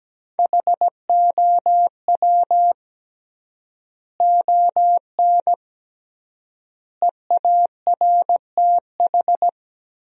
Morse code, heard as a keyed tone, about 17 wpm, 700 Hz. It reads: HOW ON EARTH
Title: 3x Word-Spacing - Sentences from Top 200 Words - 17wpm